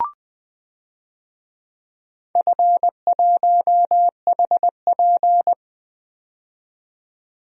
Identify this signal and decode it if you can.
Morse code, heard as a keyed tone, about 20 wpm, 700 Hz. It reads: F1HP
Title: DXCC-100 Call Signs - 4 Characters - 20wpm